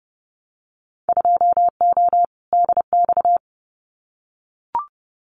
Morse code, heard as a keyed tone, about 30 wpm, 700 Hz. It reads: 2O DX
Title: Winter Field Day Exchanges Encore 30wpm